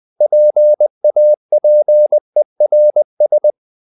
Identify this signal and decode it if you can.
Morse code, heard as a keyed tone, about 20 wpm, 600 Hz. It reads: PAPERS
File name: the old man and the sea part 2 20wpm